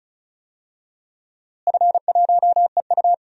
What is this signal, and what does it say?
Morse code, heard as a keyed tone, about 36 wpm, 700 Hz. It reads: F1EU
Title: DXCC-100 Call Signs - 4 Characters - Repeated 2x - 35wpm